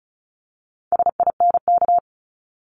Morse code, heard as a keyed tone, about 36 wpm, 700 Hz. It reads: SINK
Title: Multi-letter Phonemes 35wpm